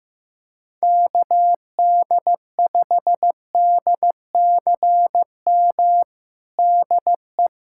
Morse code, heard as a keyed tone, about 16 wpm, 700 Hz. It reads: KD5DCM DE
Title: USA Callsign DE Callsign 15wpm